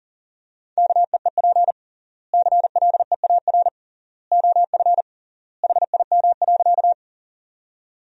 Morse code, heard as a keyed tone, about 40 wpm, 700 Hz. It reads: KEEP CLEAR OF HIM.
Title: 3x Word-Spacing - Sentences from Top 500 Words - 40wpm